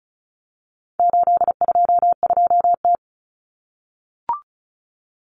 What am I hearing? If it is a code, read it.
Morse code, heard as a keyed tone, about 34 wpm, 700 Hz. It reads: 822T
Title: Common Cut Numbers - 4-Digit 35wpm